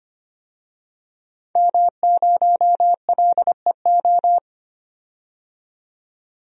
Morse code, heard as a keyed tone, about 25 wpm, 700 Hz. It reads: M0LEO
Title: DXCC-100 Call Signs - 5 Characters - 25wpm